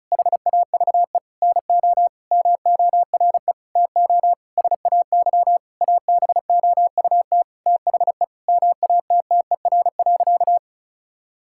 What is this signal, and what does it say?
Morse code, heard as a keyed tone, about 36 wpm, 700 Hz. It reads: HAVE NO MORE TO SAY ABOUT THE MATTER.
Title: Sentences from Top 1000 Words - Review Part 5 - 35wpm